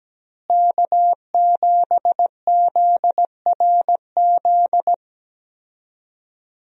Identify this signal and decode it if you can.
Morse code, heard as a keyed tone, about 17 wpm, 700 Hz. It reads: K7ZRZ
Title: USA Call Signs - 5 Characters - Repeated 2x - 17wpm